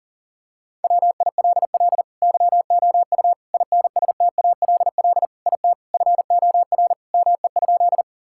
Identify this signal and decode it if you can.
Morse code, heard as a keyed tone, about 40 wpm, 700 Hz. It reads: WILL YOU INSTALL IT FOR ME?
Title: Sentences from Top 1600 Words 40wpm